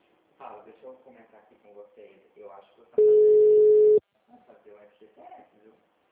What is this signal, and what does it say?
Ringback tone (Europe, 425 Hz), with speech.
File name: out- -2001- - - .5967